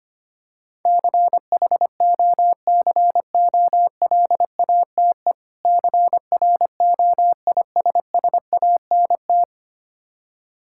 Morse code, heard as a keyed tone, about 25 wpm, 700 Hz. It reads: CHOCOLATE CROISSANT
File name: practice / 25wpm / Difficult Words with Hints - 25wpm